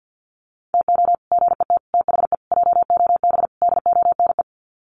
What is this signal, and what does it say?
Morse code, heard as a keyed tone, about 50 wpm, 700 Hz. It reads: TO GET THE JOB DONE
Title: Sets of 5 Words Encore2 50wpm